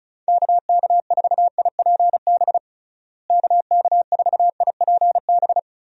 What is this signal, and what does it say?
Morse code, heard as a keyed tone, about 36 wpm, 700 Hz. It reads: KK4IPB KK4IPB
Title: USA Call Signs - 6 Characters - Repeated 2x - 35wpm